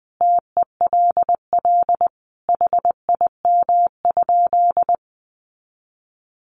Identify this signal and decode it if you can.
Morse code, heard as a keyed tone, about 20 wpm, 700 Hz. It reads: TELL HIM?
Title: Sentences from Top 200 Words 20wpm